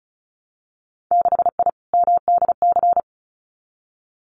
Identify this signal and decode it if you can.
Morse code, heard as a keyed tone, about 34 wpm, 700 Hz. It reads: 6I MDC
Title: Winter Field Day Exchanges Encore 35wpm